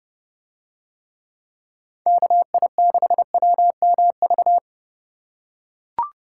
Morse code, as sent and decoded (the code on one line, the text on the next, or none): -.- .. -.... .-- -- ...-
KI6WMV